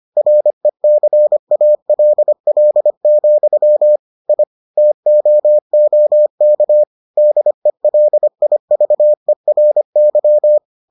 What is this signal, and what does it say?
Morse code, heard as a keyed tone, about 25 wpm, 600 Hz. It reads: RECALL, I TOOK DELIVERY